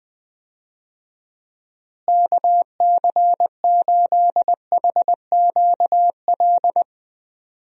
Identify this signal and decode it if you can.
Morse code, heard as a keyed tone, about 20 wpm, 700 Hz. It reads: KC8HQL
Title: USA Call Signs - 6 Characters - Repeated 2x - 20wpm